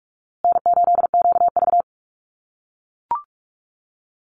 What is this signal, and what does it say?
Morse code, heard as a keyed tone, about 44 wpm, 700 Hz. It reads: N8QV
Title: USA Call Signs Repeated 2x 45wpm